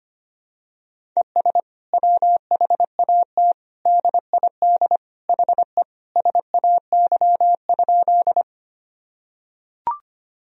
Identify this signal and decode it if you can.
Morse code, heard as a keyed tone, about 25 wpm, 700 Hz. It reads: ES WHAT DID HE SAY?
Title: Sentences from Top 200 Words and Abbreviations 25wpm